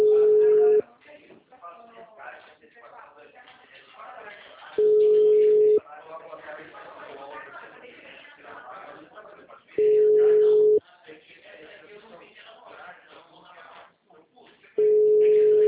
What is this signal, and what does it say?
Ringback tone (Europe, 425 Hz), with speech.
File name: out- -2017- - - .87652